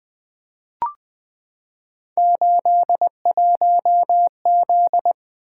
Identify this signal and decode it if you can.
Morse code, heard as a keyed tone, about 20 wpm, 700 Hz. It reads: 81Z
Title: Three Letter-Number Combinations - 20wpm